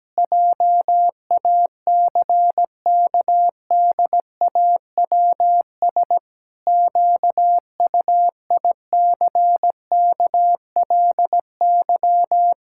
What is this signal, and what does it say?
Morse code, heard as a keyed tone, about 17 wpm, 700 Hz. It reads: JACKDAWS QUICKLY
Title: Pangram Challenge v2 - 17wpm